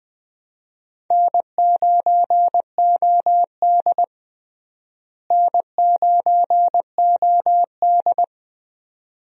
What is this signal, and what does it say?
Morse code, heard as a keyed tone, about 21 wpm, 700 Hz. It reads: N9OD N9OD
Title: USA Call Signs - 4 Characters - Repeated 2x - 20wpm